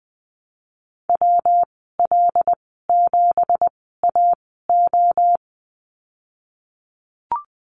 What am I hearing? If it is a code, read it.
Morse code, heard as a keyed tone, about 19 wpm, spaced as for 14 wpm, 700 Hz. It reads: WL7AO
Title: 179 Lesson 29g - Slash - Call Signs 20wpm